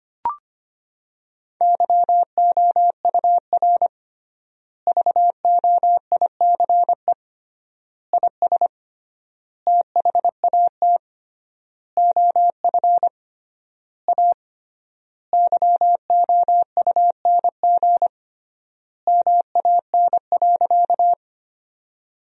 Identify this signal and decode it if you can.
Morse code, heard as a keyed tone, about 25 wpm, 700 Hz. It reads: YOUR VOICE IS THAT OF A YOUNG MAN.
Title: 3x Word-Spacing - Sentences from Top 500 Words - 25wpm